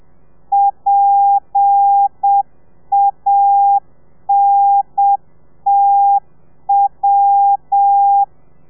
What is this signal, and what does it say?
Morse code, heard as a keyed tone, about 7 wpm, 800 Hz. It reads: PANTW
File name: telegrafia / i2tfj / Lez8 30